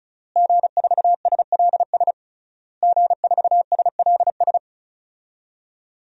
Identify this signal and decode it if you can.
Morse code, heard as a keyed tone, about 34 wpm, 700 Hz. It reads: G4SLS G4SLS
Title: DXCC-100 Call Signs - 5 Characters - Repeated 3x - 35wpm